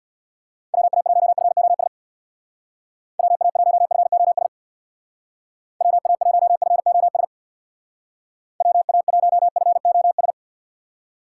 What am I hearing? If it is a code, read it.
Morse code, heard as a keyed tone, about 59 wpm, 700 Hz. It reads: WA1FOS WA1FOS WA1FOS WA1FOS
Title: Speed Racing - USA 2x3 Call Signs 50wpm